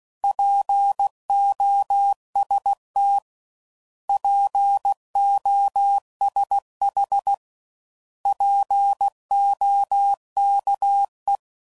Morse code, heard as a keyed tone, about 16 wpm, 800 Hz. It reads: POST POSH POKE
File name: Prefixes / By First Letter / Ps